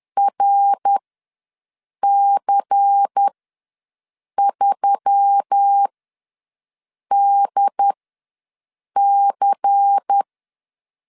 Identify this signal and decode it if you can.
Morse code, heard as a keyed tone, about 10 wpm, spaced as for 6 wpm, 800 Hz. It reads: RC3DC
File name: LEZIONI / 5 LBVC / 5 LBVC ws 10wpm 3733